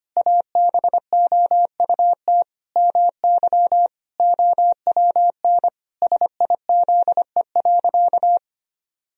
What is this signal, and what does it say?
Morse code, heard as a keyed tone, about 25 wpm, 700 Hz. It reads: ABOUT MY OWN SIZE.
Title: Sentences from Top 1000 Words - Review Part 2 - 25wpm